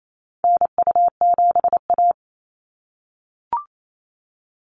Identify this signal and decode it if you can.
Morse code, heard as a keyed tone, about 28 wpm, 700 Hz. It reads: NU7A
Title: NA SOTA Activator Callsigns 28wpm